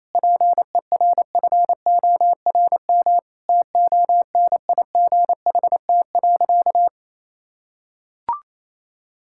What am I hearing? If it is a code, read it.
Morse code, heard as a keyed tone, about 28 wpm, 700 Hz. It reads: PERFORM TONIGHT.
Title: Sentences from Top 1000 Words - Review Part 5 - 28wpm